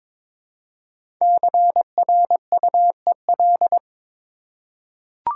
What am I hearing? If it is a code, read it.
Morse code, heard as a keyed tone, about 22 wpm, 700 Hz. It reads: CRUEL
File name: 5 Letter Words - First 3 Letters Given - 22wpm